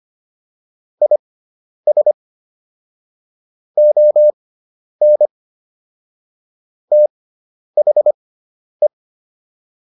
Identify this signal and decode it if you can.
Morse code, heard as a keyed tone, about 25 wpm, spaced as for 10 wpm, 600 Hz. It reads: IS ON THE